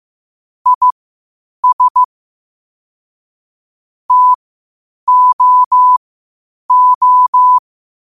Morse code, heard as a keyed tone, about 15 wpm, spaced as for 8 wpm, 1000 Hz. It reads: IS TOO
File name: qotdfiles / 8wpm